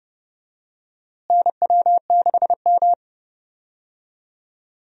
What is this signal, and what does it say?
Morse code, heard as a keyed tone, about 29 wpm, 700 Hz. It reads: NW6M
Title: USA Call Signs - 4 Characters - Repeated 3x - 30wpm